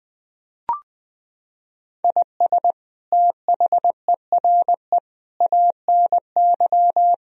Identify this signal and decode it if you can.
Morse code, heard as a keyed tone, about 20 wpm, 700 Hz. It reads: IS THERE ANY
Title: Sentences from Top 1000 Words - Review Part 4 - 20wpm